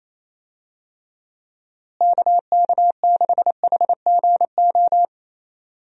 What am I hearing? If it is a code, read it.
Morse code, heard as a keyed tone, about 28 wpm, 700 Hz. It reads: KK6HGO
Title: USA Call Signs - 6 Characters - Repeated 2x - 28wpm